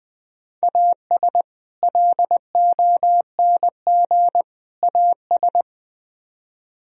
Morse code, heard as a keyed tone, about 20 wpm, 700 Hz. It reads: AS LONG AS